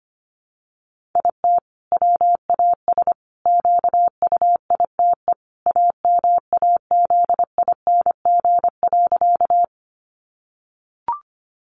Morse code, heard as a keyed tone, about 25 wpm, 700 Hz. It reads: IT WAS QUITE AMAZING.
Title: Sentences from Top 1500 Words 25wpm